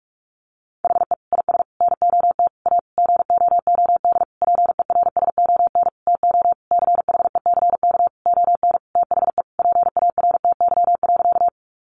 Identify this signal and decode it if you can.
Morse code, heard as a keyed tone, about 44 wpm, 700 Hz. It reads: HE IS NOT A GOOD PERSON TO CHECK ON THE PARTY.
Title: Sentences from Top 1400 Words 45wpm